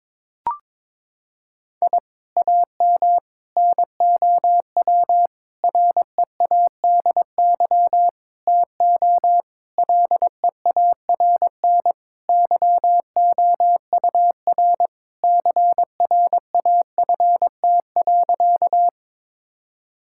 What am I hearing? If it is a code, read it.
Morse code, heard as a keyed tone, about 22 wpm, 700 Hz. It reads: I AM NOW READY TO LEARN YOUR CRAFT.
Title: Sentences from Top 1000 Words - Review Part 6 - 22wpm